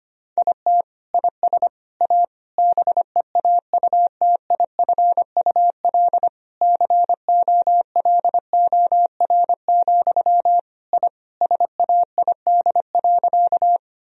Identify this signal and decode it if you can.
Morse code, heard as a keyed tone, about 25 wpm, 700 Hz. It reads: IT IS A BEAUTIFUL COLOR, I SAID.
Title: Sentences from Top 1000 Words - Review Part 2 - 25wpm